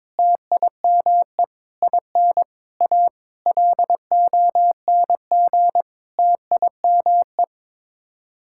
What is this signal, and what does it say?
Morse code, heard as a keyed tone, about 22 wpm, 700 Hz. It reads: TIME IN A LONG TIME
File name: practice / 22wpm / Sets of 8 Words - 22wpm